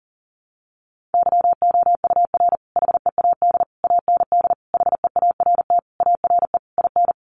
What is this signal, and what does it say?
Morse code, heard as a keyed tone, about 40 wpm, 700 Hz. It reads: YOUR HEAD AND HEART ARE IN
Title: Sentences from Top 1300 Words 40wpm